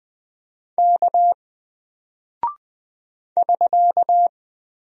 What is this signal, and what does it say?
Morse code, heard as a keyed tone, about 20 wpm, 700 Hz. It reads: K <SK>
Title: Single Character (Letter:Number:Punctuation:Prosign) Alt 20wpm